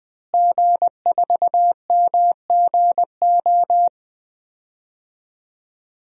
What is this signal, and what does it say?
Morse code, heard as a keyed tone, about 20 wpm, 700 Hz. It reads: G4MGO
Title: DXCC-100 Call Signs - 5 Characters - Repeated 2x - 20wpm